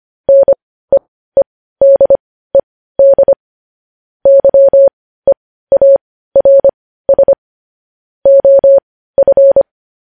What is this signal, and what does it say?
Morse code, heard as a keyed tone, about 25 wpm, spaced as for 15 wpm, 550 Hz. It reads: NEEDED YEARS OF
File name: podcast / News headlines-1x-25x15